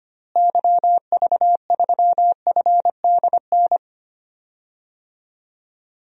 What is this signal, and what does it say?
Morse code, heard as a keyed tone, about 25 wpm, 700 Hz. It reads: YV3FDN
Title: DXCC-100 Call Signs - 6 Characters - Repeated 3x - 25wpm